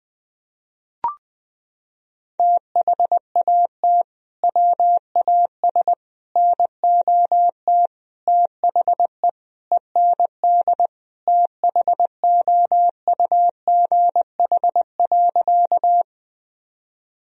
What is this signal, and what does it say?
Morse code, heard as a keyed tone, about 20 wpm, 700 Hz. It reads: THAT WAS NOT THE END THOUGH.
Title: Sentences from Top 1000 Words - Review Part 2 - 20wpm